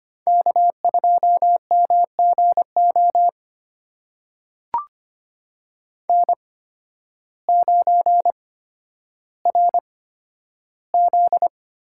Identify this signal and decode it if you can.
Morse code, heard as a keyed tone, about 25 wpm, spaced as for 12 wpm, 700 Hz. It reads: K2MGO N9RZ
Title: USA Call Signs - 5 Characters - 8x Character Spacing - 25wpm